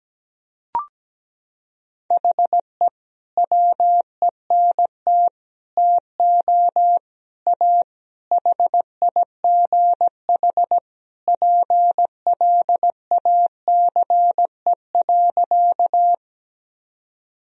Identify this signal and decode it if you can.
Morse code, heard as a keyed tone, about 17 wpm, 700 Hz. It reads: HE WENT TO A HIGH PLACE.
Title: Sentences from Top 1000 Words - Review Part 1 - 17wpm